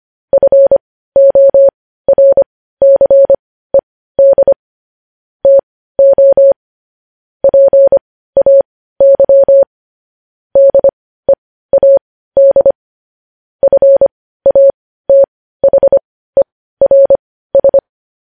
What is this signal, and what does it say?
Morse code, heard as a keyed tone, about 25 wpm, spaced as for 15 wpm, 550 Hz. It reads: FORCED TO PAY DEAD FATHERS
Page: News headlines in Morse code at 15 wpm